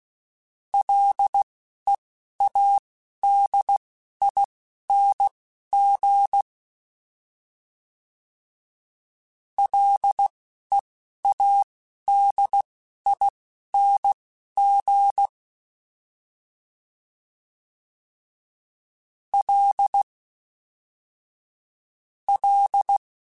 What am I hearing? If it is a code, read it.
Morse code, heard as a keyed tone, about 16 wpm, spaced as for 12 wpm, 800 Hz. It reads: LEADING LEADING L L
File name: Ls / Single L m3s / Single L Prefixe 3x2 Twice @10